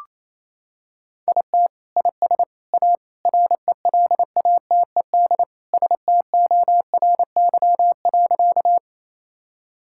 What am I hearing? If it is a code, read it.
Morse code, heard as a keyed tone, about 27 wpm, 700 Hz. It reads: IT IS A RELATED STORY.
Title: Sentences from Top 1000 Words - Review Part 5 - 28wpm